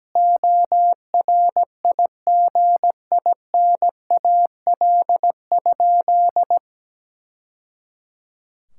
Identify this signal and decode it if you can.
Morse code, heard as a keyed tone, about 17 wpm, 700 Hz. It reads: ORIGINAL?
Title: Sentences from Top 1000 Words - Review Part 6 - 17wpm